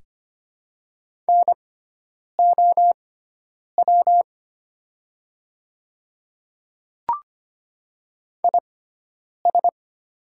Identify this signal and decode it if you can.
Morse code, heard as a keyed tone, about 25 wpm, spaced as for 9 wpm, 700 Hz. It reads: NOW IS
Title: 6x Character Spacing - Top 100 Words - 25wpm